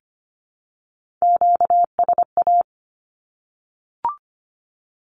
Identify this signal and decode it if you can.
Morse code, heard as a keyed tone, about 25 wpm, 700 Hz. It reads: QSA
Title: Q-Codes 25wpm